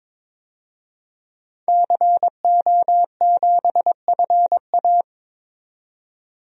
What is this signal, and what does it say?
Morse code, heard as a keyed tone, about 22 wpm, 700 Hz. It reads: CO7FA
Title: DXCC-100 Call Signs - 5 Characters - 22wpm